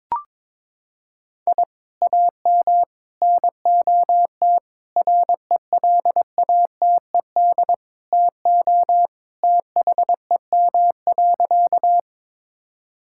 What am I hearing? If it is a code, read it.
Morse code, heard as a keyed tone, about 22 wpm, 700 Hz. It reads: I AM NOT RELATED TO THEM.
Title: Sentences from Top 1000 Words - Review Part 5 - 22wpm